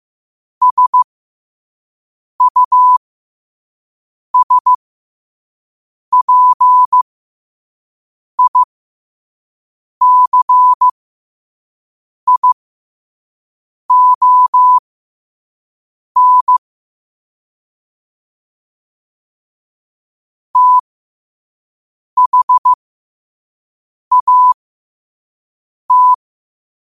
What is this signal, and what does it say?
Morse code, heard as a keyed tone, about 15 wpm, spaced as for 5 wpm, 1000 Hz. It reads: SUSPICION THAT
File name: qotdfiles / 5wpm